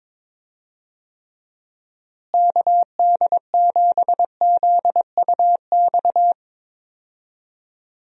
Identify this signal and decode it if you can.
Morse code, heard as a keyed tone, about 21 wpm, 700 Hz. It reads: KD7ZUX
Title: USA Call Signs - 6 Characters - Repeated 2x - 22wpm